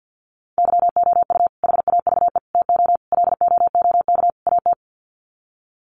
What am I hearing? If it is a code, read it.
Morse code, heard as a keyed tone, about 50 wpm, 700 Hz. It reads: YOU HAVE TO LOOK AT
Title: Sets of 5 Words Encore 50wpm